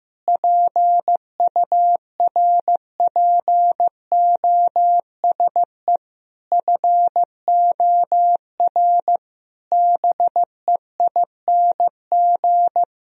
Morse code, heard as a keyed tone, about 16 wpm, 700 Hz. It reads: PURPOSE FOR BEING
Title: Sentences from Top 1000 Words - Review Part 2 - 15wpm